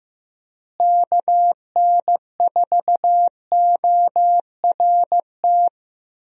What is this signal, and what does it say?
Morse code, heard as a keyed tone, about 15 wpm, 700 Hz. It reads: KN4ORT
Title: USA Call Signs - 6 Characters - Repeated 2x - 15wpm